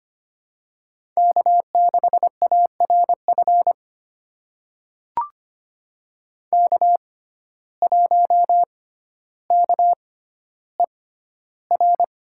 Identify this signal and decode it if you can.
Morse code, heard as a keyed tone, about 25 wpm, spaced as for 13 wpm, 700 Hz. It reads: K6ARF K1KER
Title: USA Call Signs - 5 Characters - 6x Character Spacing - 25wpm